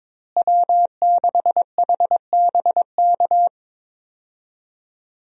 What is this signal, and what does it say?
Morse code, heard as a keyed tone, about 22 wpm, 700 Hz. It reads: W6HBK
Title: USA Call Signs - 5 Characters - Repeated 3x - 22wpm